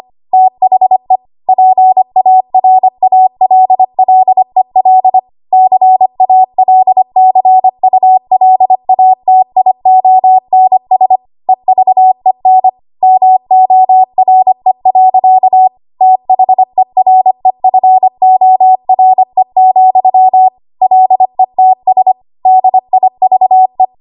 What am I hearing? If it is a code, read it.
Morse code, heard as a keyed tone, about 25 wpm, 750 Hz. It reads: THE PARALLEL CALCULATIONS EVEN MORE. THEREFORE, LETS DIVE
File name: Archive / 25 WPM / 25WPM